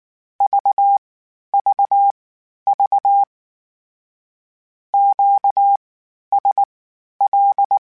Morse code, heard as a keyed tone, about 19 wpm, spaced as for 11 wpm, 800 Hz. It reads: VVV QSL
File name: (spaced) / Q-Calls Spaced-mp3s / Q Calls Twice sp @18